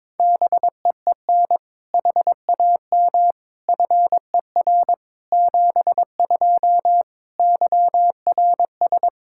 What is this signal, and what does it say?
Morse code, heard as a keyed tone, about 22 wpm, 700 Hz. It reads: BEEN HAM FER 72 YRS
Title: Practice QSO - Exchange 3 - More Info - 22wpm